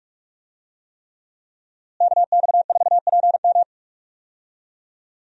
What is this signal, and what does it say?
Morse code, heard as a keyed tone, about 42 wpm, 700 Hz. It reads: KX4PM
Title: USA Call Signs - 5 Characters - Repeated 3x - 45wpm